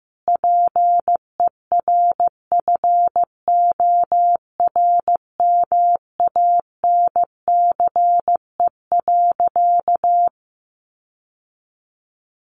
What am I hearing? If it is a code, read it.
Morse code, heard as a keyed tone, about 15 wpm, 700 Hz. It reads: PERFORMANCE.
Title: Sentences from Top 1100 Words 15wpm